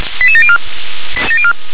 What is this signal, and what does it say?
Five-tone selective call (ZVEI), heard as quick runs of five tones, tones of about 70 ms. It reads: 88704